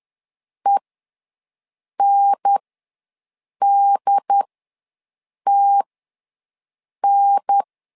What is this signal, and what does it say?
Morse code, heard as a keyed tone, about 10 wpm, spaced as for 6 wpm, 800 Hz. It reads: ENDTN
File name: 2 UHDN ws 10wpm